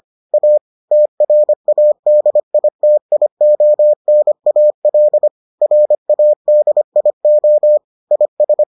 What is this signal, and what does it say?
Morse code, heard as a keyed tone, about 25 wpm, 600 Hz. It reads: A TRADITIONAL RADIO IS